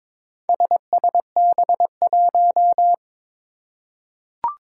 Morse code, heard as a keyed tone, about 22 wpm, 700 Hz. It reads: SSB1
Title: Four Challenging Letter-Number Combinations 22wpm